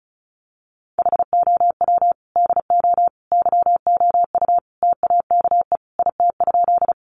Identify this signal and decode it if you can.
Morse code, heard as a keyed tone, about 34 wpm, 700 Hz. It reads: HOW DO YOU TAKE IT?
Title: Sentences from Top 200 Words 35wpm